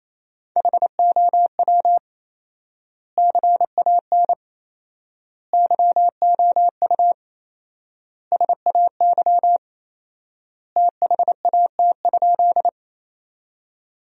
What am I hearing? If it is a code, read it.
Morse code, heard as a keyed tone, about 27 wpm, 700 Hz. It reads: HOW CAN YOU SAY THAT?
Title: 4x Word-Spacing - Sentences from Top 200 Words - 28wpm